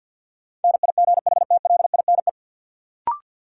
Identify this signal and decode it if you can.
Morse code, heard as a keyed tone, about 50 wpm, 700 Hz. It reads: NIGHTLINE
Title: Sets of 5 Words - New Word Drill - 50wpm